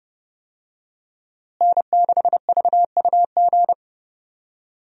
Morse code, heard as a keyed tone, about 32 wpm, 700 Hz. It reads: N6VUG
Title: USA Call Signs - 5 Characters - Repeated 2x - 30wpm